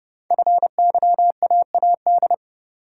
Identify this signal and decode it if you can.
Morse code, heard as a keyed tone, about 28 wpm, 690 Hz. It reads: FYAAD
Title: Code Groups - 5 Letters - 30wpm